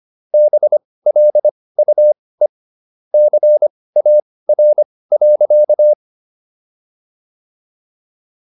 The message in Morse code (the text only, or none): BLUE CAR.